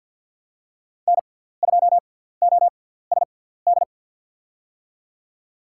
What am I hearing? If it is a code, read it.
Morse code, heard as a keyed tone, about 48 wpm, spaced as for 17 wpm, 700 Hz. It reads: N2OSD
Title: USA Call Signs - 5 Characters - 6x Character Spacing - 50wpm